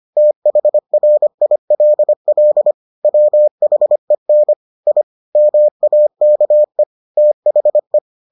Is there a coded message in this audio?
Yes: Morse code from 0.2 to 8.0 s, heard as a keyed tone, about 25 wpm, 600 Hz. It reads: THRILL WHEN I MAKE THE